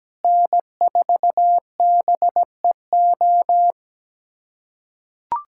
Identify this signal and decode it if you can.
Morse code, heard as a keyed tone, about 17 wpm, 700 Hz. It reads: N4BEO
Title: USA Callsign DE Callsign 17wpm